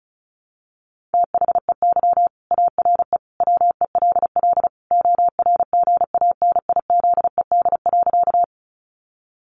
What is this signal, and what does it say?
Morse code, heard as a keyed tone, about 34 wpm, 700 Hz. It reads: THEY ARE WELL ORGANIZED.
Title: Sentences from Top 1300 Words 35wpm